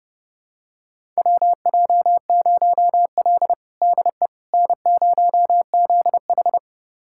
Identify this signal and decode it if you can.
Morse code, heard as a keyed tone, about 30 wpm, 700 Hz. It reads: WJ0L DE N0ZH
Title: USA Callsign DE Callsign - 4 Characters - 30wpm